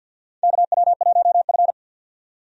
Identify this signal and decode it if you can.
Morse code, heard as a keyed tone, about 46 wpm, 700 Hz. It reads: KW1F
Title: USA Call Signs - 4 Characters - 50wpm